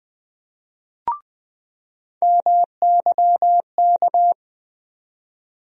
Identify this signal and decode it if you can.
Morse code, heard as a keyed tone, about 20 wpm, 700 Hz. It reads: MYK
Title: USA Call Sign Postfixes - 3 Letters 20wpm